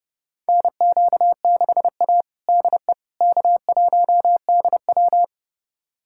Morse code, heard as a keyed tone, about 30 wpm, 700 Hz. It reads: NQ6A DE K1DW
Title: USA Callsign DE Callsign - 4 Characters - 30wpm